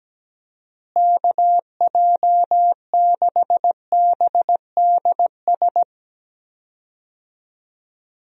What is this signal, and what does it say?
Morse code, heard as a keyed tone, about 17 wpm, 700 Hz. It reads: KJ6BDS
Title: USA Call Signs - 6 Characters - Repeated 2x - 17wpm